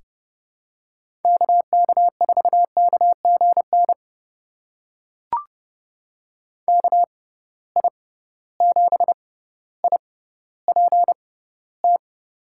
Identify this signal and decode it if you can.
Morse code, heard as a keyed tone, about 30 wpm, spaced as for 15 wpm, 700 Hz. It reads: KK4KGN KI7IPT
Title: USA Call Signs - 6 Characters - 6x Character Spacing - 30wpm